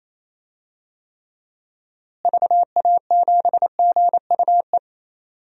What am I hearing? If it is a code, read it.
Morse code, heard as a keyed tone, about 27 wpm, 700 Hz. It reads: VA7GUE
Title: DXCC-100 Call Signs - 6 Characters - Repeated 3x - 28wpm